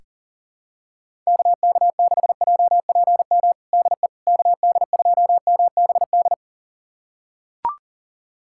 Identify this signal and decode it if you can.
Morse code, heard as a keyed tone, about 40 wpm, 690 Hz. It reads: KK6JPM DE KD2MBD
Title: USA Callsign DE Callsign - 6 Characters - 40wpm